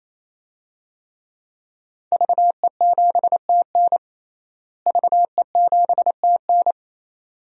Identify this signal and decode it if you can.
Morse code, heard as a keyed tone, about 28 wpm, 690 Hz. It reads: VE7TN VE7TN
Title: DXCC-100 Call Signs - 5 Characters - Repeated 3x - 28wpm